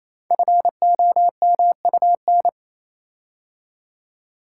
Morse code, heard as a keyed tone, about 27 wpm, 690 Hz. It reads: FOMUN
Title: Code Groups - 5 Letters - 28wpm